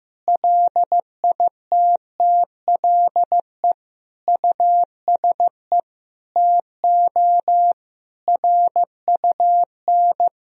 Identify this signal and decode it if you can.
Morse code, heard as a keyed tone, about 15 wpm, 700 Hz. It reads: LITTLE USE TO RUN
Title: Sentences from Top 1000 Words - Review Part 1 - 15wpm